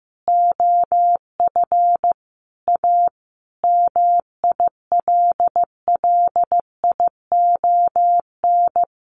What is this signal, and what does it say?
Morse code, heard as a keyed tone, about 15 wpm, 700 Hz. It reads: OF A MILLION
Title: Sets of 5 Words Encore2 15wpm